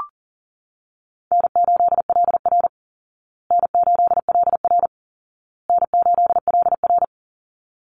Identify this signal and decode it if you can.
Morse code, heard as a keyed tone, about 37 wpm, 700 Hz. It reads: N8LR N8LR N8LR
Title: Easy Call Signs Repeated 3x - 40wpm